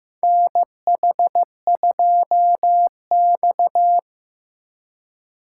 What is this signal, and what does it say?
Morse code, heard as a keyed tone, about 16 wpm, 700 Hz. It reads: NH2X
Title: USA Call Signs - 4x Character Spacing - 15wpm